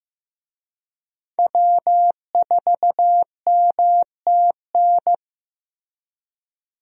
Morse code, heard as a keyed tone, about 15 wpm, 700 Hz. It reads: W4MTN
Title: USA Call Signs - 5 Characters - Repeated 2x - 15wpm